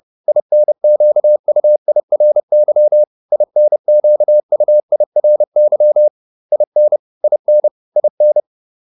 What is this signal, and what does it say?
Morse code, heard as a keyed tone, about 30 wpm, 600 Hz. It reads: INQUIRY INQUIRY IN IN IN